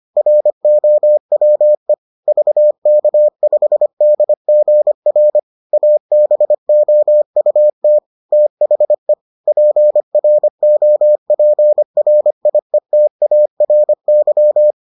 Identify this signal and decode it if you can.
Morse code, heard as a keyed tone, about 25 wpm, 600 Hz. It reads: ROWE VK5DGR ABOUT THE PROPRIETARY